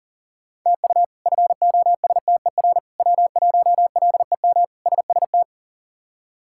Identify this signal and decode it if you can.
Morse code, heard as a keyed tone, about 39 wpm, 700 Hz. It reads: TU FOSTER W1LEM SST
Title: K1USN SST QSO Practice - 40wpm